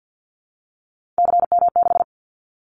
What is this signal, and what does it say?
Morse code, heard as a keyed tone, about 50 wpm, 700 Hz. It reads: CM6
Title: Callsign Prefixes for Top 100 DXCC - 50wpm